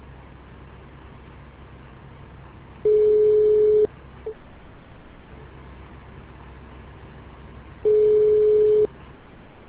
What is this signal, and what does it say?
Ringback tone (Europe, 425 Hz).